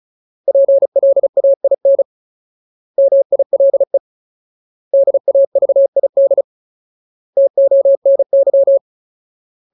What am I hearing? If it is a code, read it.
Morse code, heard as a keyed tone, about 34 wpm, 550 Hz. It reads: PLAIN MILE DAVID TONY
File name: Ultimate Word List / 35WPM Ultimate